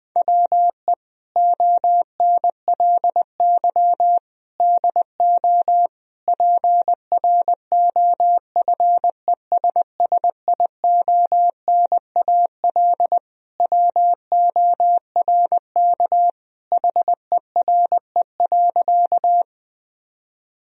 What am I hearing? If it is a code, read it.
Morse code, heard as a keyed tone, about 21 wpm, 700 Hz. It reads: WE ONLY DO PROFESSIONAL WORK HERE.
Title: Sentences from Top 1000 Words - Review Part 4 - 20wpm